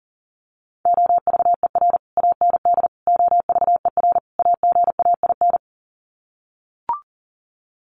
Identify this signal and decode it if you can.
Morse code, heard as a keyed tone, about 40 wpm, 700 Hz. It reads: OVER AND OVER AGAIN
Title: Sets of 4 Words 40wpm